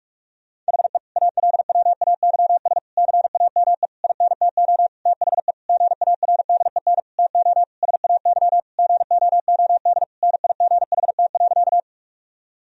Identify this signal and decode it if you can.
Morse code, heard as a keyed tone, about 46 wpm, 700 Hz. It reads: HE ALWAYS CAME INTO THE GARDEN TO SAY GOOD NIGHT.
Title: Sentences from Top 1000 Words - Review Part 4 - 45wpm